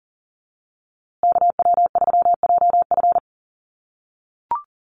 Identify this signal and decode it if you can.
Morse code, heard as a keyed tone, about 37 wpm, 700 Hz. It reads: KW3JF
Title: USA Call Signs Repeated 3x 40wpm